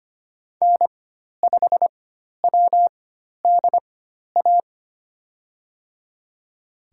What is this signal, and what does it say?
Morse code, heard as a keyed tone, about 25 wpm, spaced as for 12 wpm, 700 Hz. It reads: N5WDA
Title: USA Call Signs - 5 Characters - 4x Character Spacing - 25wpm